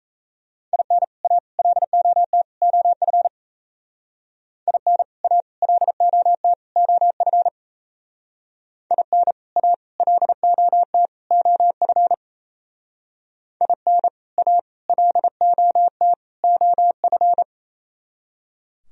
Morse code, heard as a keyed tone, about 35 wpm, 700 Hz. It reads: IN A LOT OF IN A LOT OF IN A LOT OF IN A LOT OF
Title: Speed-Racing - Sets of 4 Words 28wpm